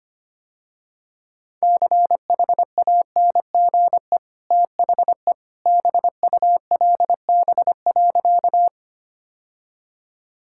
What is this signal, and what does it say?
Morse code, heard as a keyed tone, about 25 wpm, 690 Hz. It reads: CHANGE THE BULB.
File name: Easy Sentences with Hints - 25wpm